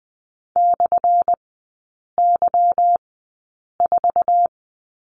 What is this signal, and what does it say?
Morse code, heard as a keyed tone, about 20 wpm, spaced as for 8 wpm, 700 Hz. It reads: /Y4
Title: 178 Lesson 29e - Slash - Three Characters 20wpm